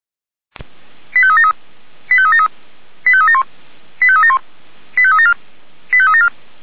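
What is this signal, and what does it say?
Five-tone selective call (ZVEI), heard as quick runs of five tones, tones of about 80 ms. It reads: 85373 85373 85371 85371 85374 85374
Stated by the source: ZVEI five-tone call received by SDR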